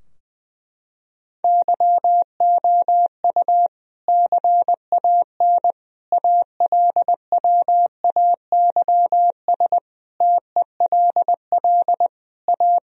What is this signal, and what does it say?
Morse code, heard as a keyed tone, about 20 wpm, 700 Hz. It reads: YOU CAN ALWAYS TELL A
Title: Sentences from Top 1000 Words - Review Part 4 - 20wpm